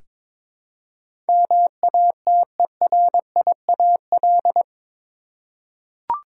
Morse code, heard as a keyed tone, about 22 wpm, 700 Hz. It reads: MATERIAL
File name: practice / 22wpm / Eight Letter Words - 22wpm